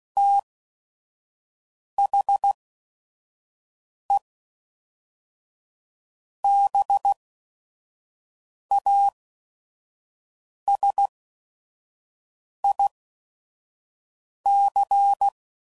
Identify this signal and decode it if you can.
Morse code, heard as a keyed tone, about 16 wpm, spaced as for 5 wpm, 800 Hz. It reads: THE BASIC
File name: CW Code Groups @5WPM / Technical Subjects / Gates Intro